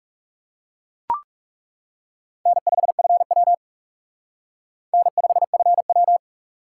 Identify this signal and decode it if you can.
Morse code, heard as a keyed tone, about 37 wpm, 700 Hz. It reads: N5FW N5FW
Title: Speed Racing - USA 1x2 Call Signs 30wpm